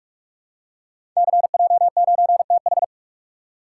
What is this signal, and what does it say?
Morse code, heard as a keyed tone, about 44 wpm, 700 Hz. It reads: CJ9TH
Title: Code Groups - 5 Letter:Numbers - 45wpm